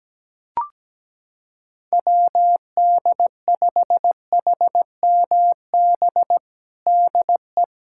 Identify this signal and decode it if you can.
Morse code, heard as a keyed tone, about 17 wpm, 700 Hz. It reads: WD5HMB DE
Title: USA Callsign DE Callsign 17wpm